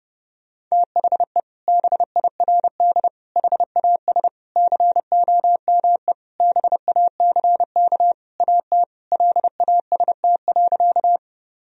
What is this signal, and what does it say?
Morse code, heard as a keyed tone, about 30 wpm, 700 Hz. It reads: THE BIRD HAS COME BACK AT LAST.
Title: Sentences from Top 1000 Words - Review Part 2 - 30wpm